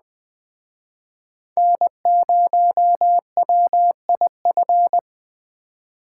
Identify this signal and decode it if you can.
Morse code, heard as a keyed tone, about 20 wpm, 700 Hz. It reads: N0WIF
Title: Speed Racing - USA Call Signs 17wpm